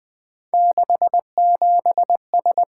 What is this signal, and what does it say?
Morse code, heard as a keyed tone, about 20 wpm, 700 Hz. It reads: 67S
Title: Three Letter-Number Combinations - 20wpm